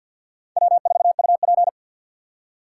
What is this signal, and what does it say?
Morse code, heard as a keyed tone, about 63 wpm, 700 Hz. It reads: W4UP